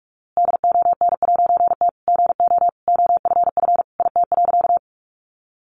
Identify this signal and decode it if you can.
Morse code, heard as a keyed tone, about 44 wpm, 700 Hz. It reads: DON'T GO OFF IT.
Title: A Sound Of Thunder 45wpm